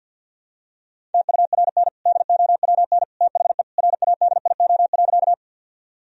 Morse code, heard as a keyed tone, about 52 wpm, 700 Hz. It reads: TURN DOWN THE RADIO.
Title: Sentences from Top 1000 Words - Review Part 3 - 50wpm